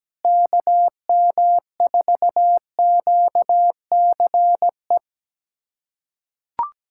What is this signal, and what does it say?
Morse code, heard as a keyed tone, about 17 wpm, 700 Hz. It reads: KM4QCE
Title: USA Callsign DE Callsign 17wpm